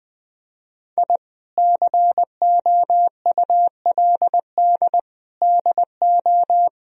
Morse code, heard as a keyed tone, about 20 wpm, 700 Hz. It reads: I COULD DO
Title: Sentences from Top 1000 Words - Review Part 2 - 20wpm